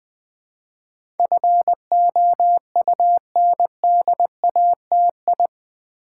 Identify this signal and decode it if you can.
Morse code, heard as a keyed tone, about 20 wpm, 700 Hz. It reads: FOUNDATI
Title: Build A Word - 4 Letters At A Time - 10-Letter Words - 20wpm